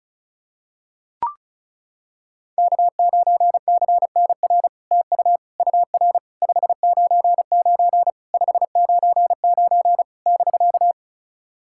Morse code, heard as a keyed tone, about 36 wpm, 700 Hz. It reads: K9CNR TU UR 599 599 <BK>
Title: SOTA QSO Practice 35wpm